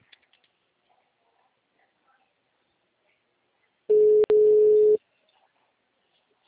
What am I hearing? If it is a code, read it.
Ringback tone (Europe, 425 Hz).
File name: out- -2016- - - .45393